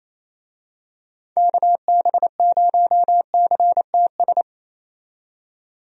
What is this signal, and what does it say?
Morse code, heard as a keyed tone, about 28 wpm, 700 Hz. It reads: KB0CTS
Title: USA Call Signs - 6 Characters - 28wpm